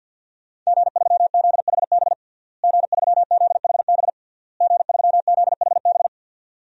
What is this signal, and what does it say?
Morse code, heard as a keyed tone, about 49 wpm, 700 Hz. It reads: G3ZHB G3ZHB G3ZHB
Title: DXCC-100 Call Signs - 5 Characters - Repeated 3x - 50wpm